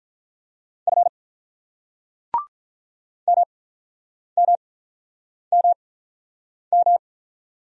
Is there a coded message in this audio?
Yes: Morse code from 0.9 to 7.0 s, heard as a keyed tone, about 46 wpm, spaced as for 9 wpm, 690 Hz. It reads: F MMMM